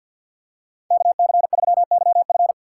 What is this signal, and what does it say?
Morse code, heard as a keyed tone, about 47 wpm, 700 Hz. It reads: KC3YF
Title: USA Call Signs - 5 Characters - 2x Character Spacing - 50wpm